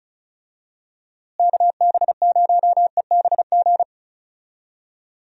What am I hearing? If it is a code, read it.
Morse code, heard as a keyed tone, about 36 wpm, 700 Hz. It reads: KB0EBG
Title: USA Call Signs - 6 Characters - Repeated 2x - 35wpm